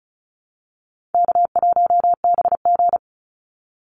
Morse code, heard as a keyed tone, about 34 wpm, 700 Hz. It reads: K1BG
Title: Easy Call Signs Repeated 3x - 35wpm